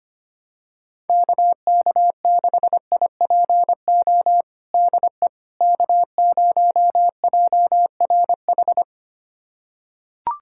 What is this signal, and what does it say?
Morse code, heard as a keyed tone, about 25 wpm, 700 Hz. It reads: KK6IPO DE K0JRH
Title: USA Callsign DE Callsign 25wpm